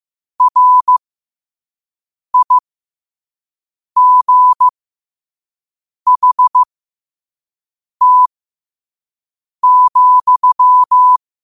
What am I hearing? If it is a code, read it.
Morse code, heard as a keyed tone, about 15 wpm, spaced as for 5 wpm, 1000 Hz. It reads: RIGHT,
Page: learn-to-code